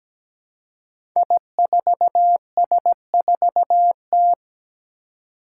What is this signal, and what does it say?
Morse code, heard as a keyed tone, about 17 wpm, 700 Hz. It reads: I4S4T
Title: Five Challenging Letter-Number Combinations 17wpm